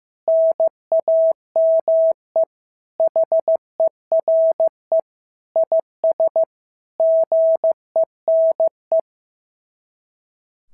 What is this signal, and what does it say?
Morse code, heard as a keyed tone, about 14 wpm, 650 Hz. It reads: NAME HERE IS GENE
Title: SOTA QSO Practice 15wpm